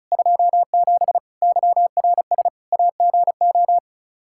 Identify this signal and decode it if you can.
Morse code, heard as a keyed tone, about 36 wpm, 700 Hz. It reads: 27 YRS AGO
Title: Practice QSO - Exchange 3 - More Info - 35wpm